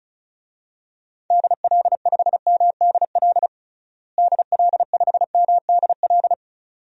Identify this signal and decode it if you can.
Morse code, heard as a keyed tone, about 36 wpm, 700 Hz. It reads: DL5MDL DL5MDL
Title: DXCC-100 Call Signs - 6 Characters - Repeated 2x - 35wpm